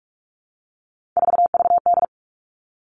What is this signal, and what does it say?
Morse code, heard as a keyed tone, about 44 wpm, 700 Hz. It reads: 4VD
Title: All Cut Numbers - 3-Digit 45wpm